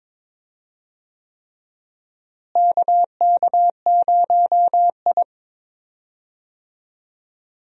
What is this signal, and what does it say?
Morse code, heard as a keyed tone, about 22 wpm, 700 Hz. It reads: KK0I
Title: USA Call Signs - 4 Characters - Repeated 3x - 22wpm